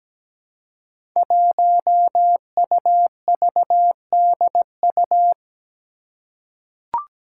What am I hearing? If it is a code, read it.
Morse code, heard as a keyed tone, about 17 wpm, 700 Hz. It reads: 1UVDU
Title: Five Challenging Letter-Number Combinations 17wpm